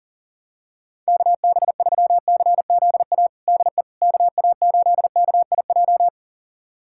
Morse code, heard as a keyed tone, about 40 wpm, 700 Hz. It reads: KB3CZA DE KA8KIJ
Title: USA Callsign DE Callsign - 6 Characters - 40wpm